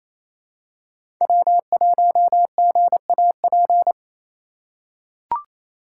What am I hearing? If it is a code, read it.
Morse code, heard as a keyed tone, about 28 wpm, 700 Hz. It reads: W1GAP
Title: USA Call Signs - 5 Characters - 4x Character Spacing - 28wpm